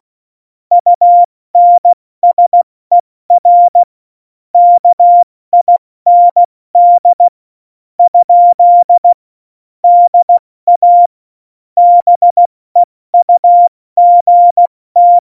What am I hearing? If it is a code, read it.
Morse code, heard as a keyed tone, about 16 wpm, spaced as for 14 wpm, 700 Hz. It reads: UNSER KIND? DA BEUGT
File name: Morsekurs / Witze / Witz 16. Oktober 2018, 70 F80